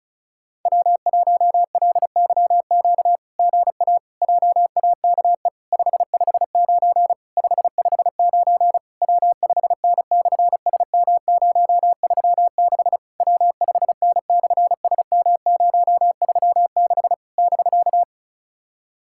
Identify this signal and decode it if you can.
Morse code, heard as a keyed tone, about 36 wpm, 700 Hz. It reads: W1LYQ GA JAKE 559 559 W5N/SM036 W5N/SM036 <BK>
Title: SOTA QSO Practice 35wpm